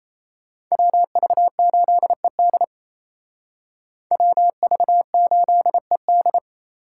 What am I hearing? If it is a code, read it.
Morse code, heard as a keyed tone, about 35 wpm, 700 Hz. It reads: WV8ED WV8ED
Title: Speed Racing - USA 2x2 Call Signs 28wpm